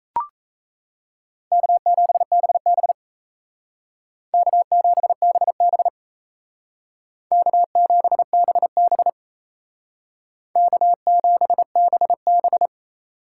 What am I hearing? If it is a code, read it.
Morse code, heard as a keyed tone, about 37 wpm, 700 Hz. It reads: K7BB K7BB K7BB K7BB